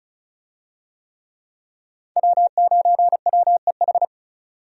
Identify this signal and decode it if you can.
Morse code, heard as a keyed tone, about 35 wpm, 700 Hz. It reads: W9WEH
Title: USA Call Signs - 5 Characters - Repeated 2x - 35wpm